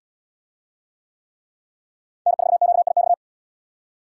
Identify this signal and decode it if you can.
Morse code, heard as a keyed tone, about 78 wpm, 700 Hz. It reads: N4QIX